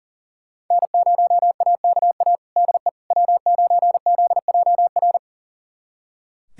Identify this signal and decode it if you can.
Morse code, heard as a keyed tone, about 40 wpm, 700 Hz. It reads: N0AKA DE W9ZJR
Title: USA Callsign DE Callsign - 5 Characters - 40wpm